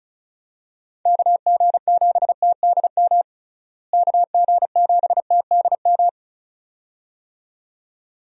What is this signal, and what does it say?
Morse code, heard as a keyed tone, about 34 wpm, 700 Hz. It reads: KG7TDM KG7TDM
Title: USA Call Signs - 6 Characters - Repeated 2x - 35wpm